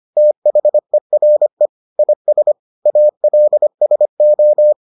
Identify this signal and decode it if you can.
Morse code, heard as a keyed tone, about 25 wpm, 600 Hz. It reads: THERE IS ALSO